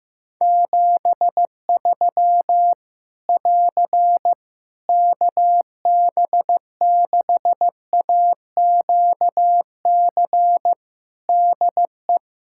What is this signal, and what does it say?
Morse code, heard as a keyed tone, about 15 wpm, 700 Hz. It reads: 73 <AR> KB6AQC DE
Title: Practice QSO - Exchange 4 - Ending - 15wpm